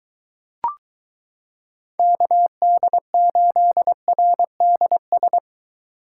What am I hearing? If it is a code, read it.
Morse code, heard as a keyed tone, about 22 wpm, 700 Hz. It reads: KD8RDS